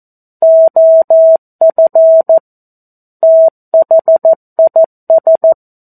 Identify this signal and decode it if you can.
Morse code, heard as a keyed tone, about 14 wpm, 650 Hz. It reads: OF THIS
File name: 02-Chapter-02 14 WPM